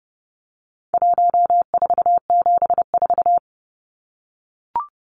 Morse code, heard as a keyed tone, about 30 wpm, 700 Hz. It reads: 1474
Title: Numbers - 4-Digit 30wpm